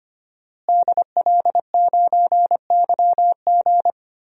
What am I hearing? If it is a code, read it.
Morse code, heard as a keyed tone, about 25 wpm, 700 Hz. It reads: DL9YG
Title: DXCC-100 Call Signs - 5 Characters - Repeated 2x - 25wpm